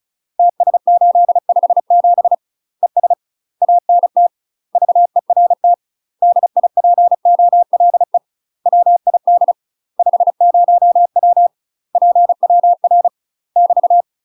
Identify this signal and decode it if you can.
Morse code, heard as a keyed tone, about 36 wpm, 700 Hz. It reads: TS857 ES ANT VERT DIPOLE WID 50W PWR <BT>
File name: UK MASTER QSOs TRAINING / 35WPM / UK MASTER QSOs 0032